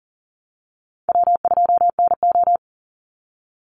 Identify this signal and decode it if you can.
Morse code, heard as a keyed tone, about 40 wpm, 700 Hz. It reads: W2NO
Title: Easy Call Signs - 40wpm